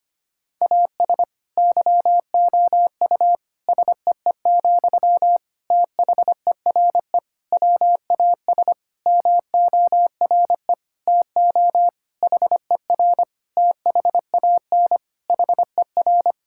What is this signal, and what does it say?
Morse code, heard as a keyed tone, about 25 wpm, 700 Hz. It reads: AS YOU SEE, THERE WAS MORE TO HER THAN HER
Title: Sentences from Top 1000 Words - Review Part 6 - 25wpm